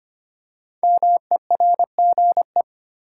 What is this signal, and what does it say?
Morse code, heard as a keyed tone, about 25 wpm, 700 Hz. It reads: MERGE
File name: 5 Letter Words - First 3 Letters Given - 25wpm